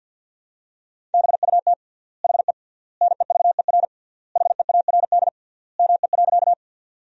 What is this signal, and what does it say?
Morse code, heard as a keyed tone, about 50 wpm, 700 Hz. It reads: BUT HE NEVER HEARD ME.
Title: 3x Word-Spacing - Sentences from Top 400 Words - 50wpm